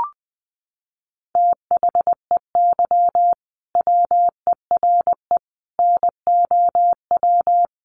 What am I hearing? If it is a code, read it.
Morse code, heard as a keyed tone, about 20 wpm, 700 Hz. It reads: THEY WERE NOW
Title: Sentences from Top 1400 Words 20wpm